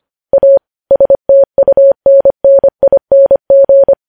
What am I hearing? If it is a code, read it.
Morse code, heard as a keyed tone, about 25 wpm, 550 Hz. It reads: A STUNNING
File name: podcast / News headlines-1x-25x25